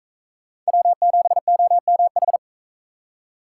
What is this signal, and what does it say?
Morse code, heard as a keyed tone, about 41 wpm, 700 Hz. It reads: W7OMH